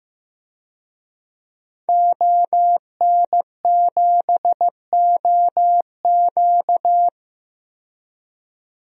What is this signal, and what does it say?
Morse code, heard as a keyed tone, about 15 wpm, 700 Hz. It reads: ON7OQ
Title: DXCC-100 Call Signs - 5 Characters - 15wpm